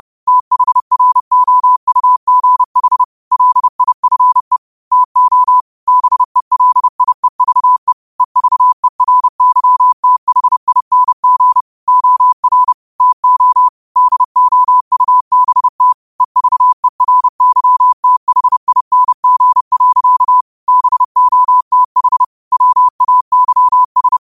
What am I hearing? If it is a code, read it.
Morse code, heard as a keyed tone, about 30 wpm, 1000 Hz. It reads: THROUGH LIFE TO BELIEVE EVERYTHING OR TO DOUBT EVERYTHING. BOTH WAYS
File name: qotdfiles / 30wpm